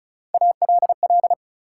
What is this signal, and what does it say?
Morse code, heard as a keyed tone, about 34 wpm, 700 Hz. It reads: ALL
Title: Sentences from Top 100 Words - New Word Drill - 35wpm